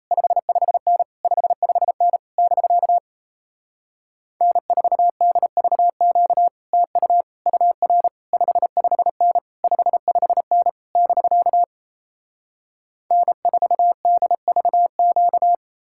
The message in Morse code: 55N 55N <BK> N4DVQ TU UR 55N 55N <BK> N4DVQ

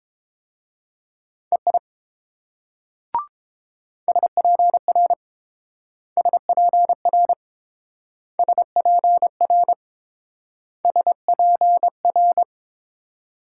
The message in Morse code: EI SPR SPR SPR SPR

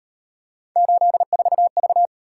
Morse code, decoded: 84V

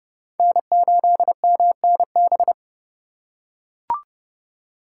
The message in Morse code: N8MNB

